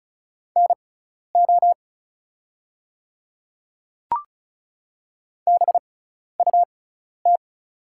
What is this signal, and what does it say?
Morse code, heard as a keyed tone, about 36 wpm, spaced as for 12 wpm, 700 Hz. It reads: NO BUT
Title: 6x Character Spacing - Top 100 Words - 35wpm